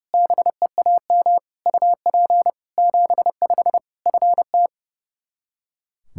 Morse code, heard as a keyed tone, about 30 wpm, 700 Hz. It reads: BEAM UP 75 FT
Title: Practice QSO - Exchange 2 - Info - 30wpm